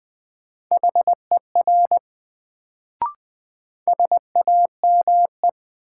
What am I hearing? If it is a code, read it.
Morse code, heard as a keyed tone, about 20 wpm, 700 Hz. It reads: HER SAME
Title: Sets of 2 Words - New Word Drill - 20wpm